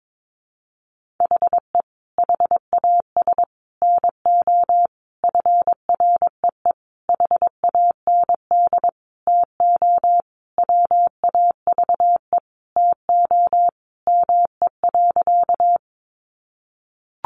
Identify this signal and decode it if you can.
Morse code, heard as a keyed tone, about 22 wpm, 700 Hz. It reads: HE HAS NO FREE HAND TO WAVE TO ME.
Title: Sentences from Top 1400 Words 22wpm